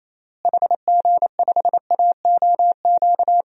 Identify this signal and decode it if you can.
Morse code, heard as a keyed tone, about 28 wpm, 700 Hz. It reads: HG5AOQ
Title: DXCC-100 Call Signs - 6 Characters - Repeated 3x - 28wpm